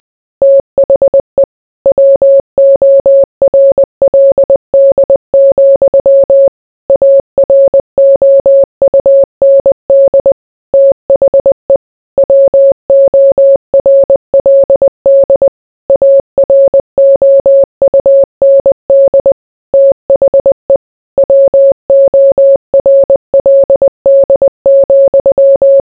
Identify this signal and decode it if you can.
Morse code, heard as a keyed tone, about 20 wpm, 550 Hz. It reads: THE WORLD, AROUND THE WORLD AROUND THE WORLD,